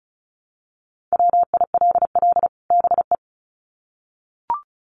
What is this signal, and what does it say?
Morse code, heard as a keyed tone, about 34 wpm, 700 Hz. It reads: WILL BE
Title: Sets of 2 Words 35wpm